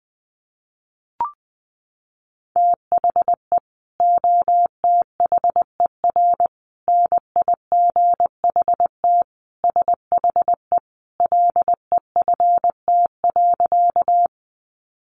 Morse code, heard as a keyed tone, about 20 wpm, 700 Hz. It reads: THE OTHER NIGHT SHE LEFT.
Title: Sentences from Top 300 Words 20wpm